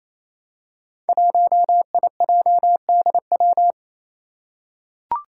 Morse code, heard as a keyed tone, about 28 wpm, 700 Hz. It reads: 1IJDW
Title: Five Challenging Letter-Number Combinations 28wpm